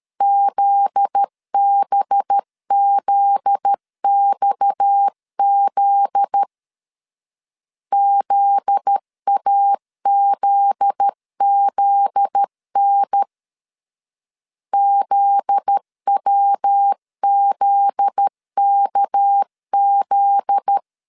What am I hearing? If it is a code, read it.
Morse code, heard as a keyed tone, about 13 wpm, 800 Hz. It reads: ZBZXZ ZAZZN ZWZKZ
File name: LEZIONI / 7 GF9Z / 7 GF9Z 12wpm 2147